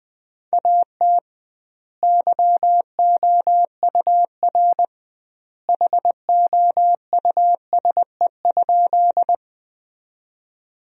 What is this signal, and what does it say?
Morse code, heard as a keyed tone, about 20 wpm, 700 Hz. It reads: AT YOUR HOUSE?
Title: 2x Word-Spacing - Sentences from Top 200 Words - 20wpm